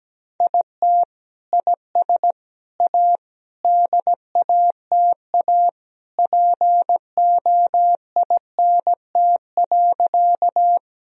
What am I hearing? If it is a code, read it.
Morse code, heard as a keyed tone, about 17 wpm, 700 Hz. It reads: IT IS A DATA POINT.
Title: Sentences from Top 1000 Words - Review Part 3 - 17wpm